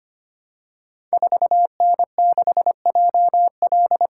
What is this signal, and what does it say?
Morse code, heard as a keyed tone, about 25 wpm, 700 Hz. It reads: 4N6JL
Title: Code Groups - 5 Letter:Numbers - 25wpm